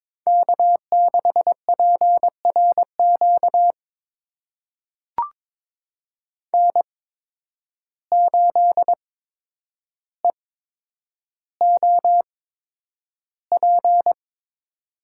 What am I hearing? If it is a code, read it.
Morse code, heard as a keyed tone, about 21 wpm, spaced as for 9 wpm, 700 Hz. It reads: K6PRQ N8EOP
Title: USA Call Signs - 5 Characters - 8x Character Spacing - 22wpm